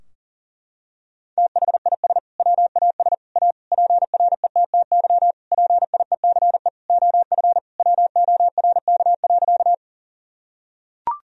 Morse code, heard as a keyed tone, about 40 wpm, 700 Hz. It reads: THIS WAS A PRETTY PIECE OF WORK.
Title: Sentences from Top 1000 Words - Review Part 2 - 40wpm